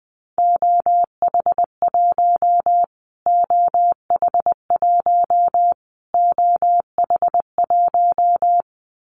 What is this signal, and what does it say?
Morse code, heard as a keyed tone, about 20 wpm, 700 Hz. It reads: OH1 OH1 OH1
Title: Callsign Prefixes for Top 25 DXCC - Repeated 3x - 20wpm